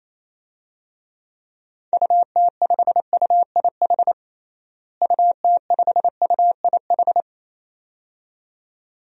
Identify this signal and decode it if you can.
Morse code, heard as a keyed tone, about 28 wpm, 700 Hz. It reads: UT5UIH UT5UIH
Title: DXCC-100 Call Signs - 6 Characters - Repeated 2x - 28wpm